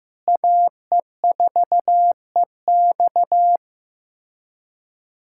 Morse code, heard as a keyed tone, about 15 wpm, 700 Hz. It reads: AE4EX
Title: USA Callsign DE Callsign - 5 Characters - 15wpm